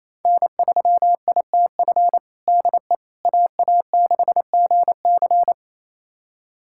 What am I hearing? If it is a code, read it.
Morse code, heard as a keyed tone, about 28 wpm, 700 Hz. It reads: N3ITF DE AA6GC
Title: USA Callsign DE Callsign - 5 Characters - 28wpm